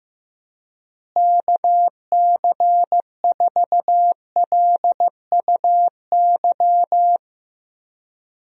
Morse code, heard as a keyed tone, about 15 wpm, 700 Hz. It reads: KC4LUY
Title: USA Call Signs - 6 Characters - Repeated 3x - 15wpm